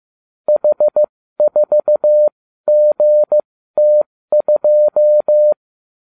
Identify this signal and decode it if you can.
Morse code, heard as a keyed tone, about 15 wpm, spaced as for 13 wpm, 600 Hz. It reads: H4GT2
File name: cw / ubacw34 3